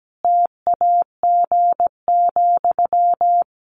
Morse code, heard as a keyed tone, about 17 wpm, 700 Hz. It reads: TAG,
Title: Additional Easy Phrases 17wpm